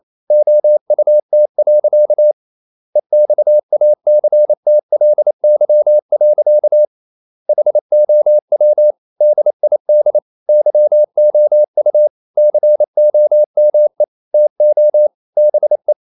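Morse code, heard as a keyed tone, about 28 wpm, 600 Hz. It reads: OUT. EXACTLY. HOW DID YOU COME TO BE
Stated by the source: Morse code practice at 28 wpm, text of Ken Follett's novel Never in English, from CW med Gnister.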